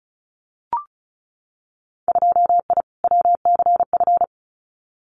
Morse code, heard as a keyed tone, about 36 wpm, 700 Hz. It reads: 2I WCF
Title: Winter Field Day Exchanges 35wpm